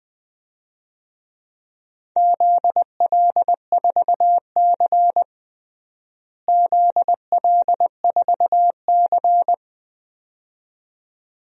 Morse code, heard as a keyed tone, about 21 wpm, 700 Hz. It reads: ZL4C ZL4C
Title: DXCC-100 Call Signs - 4 Characters - Repeated 2x - 20wpm